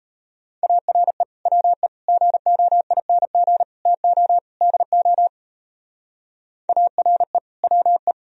Morse code, heard as a keyed tone, about 33 wpm, 700 Hz. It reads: ARE WE GOING TO DO ARE WE
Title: Speed-Racing - Sets of 5 Words 28wpm